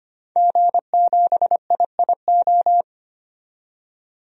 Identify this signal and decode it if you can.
Morse code, heard as a keyed tone, about 25 wpm, 700 Hz. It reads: G7IIO
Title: DXCC-100 Call Signs - 5 Characters - 25wpm